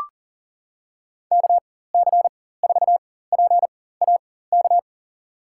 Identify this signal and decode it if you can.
Morse code, heard as a keyed tone, about 37 wpm, spaced as for 18 wpm, 700 Hz. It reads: KC4PAK
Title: USA Call Signs - 6 Characters - 4x Character Spacing - 40wpm